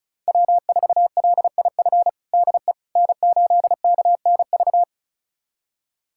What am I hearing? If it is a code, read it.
Morse code, heard as a keyed tone, about 34 wpm, 700 Hz. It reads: W4LIF DE N8KNV
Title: USA Callsign DE Callsign - 5 Characters - 35wpm